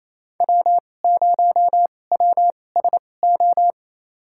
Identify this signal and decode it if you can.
Morse code, heard as a keyed tone, about 28 wpm, spaced as for 20 wpm, 700 Hz. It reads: W0WSO
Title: USA Call Signs - 5 Characters - 2x Character Spacing - 28wpm